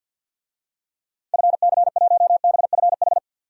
Morse code, heard as a keyed tone, about 48 wpm, 700 Hz. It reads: FC1BFH
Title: DXCC-100 Call Signs - 6 Characters - 50wpm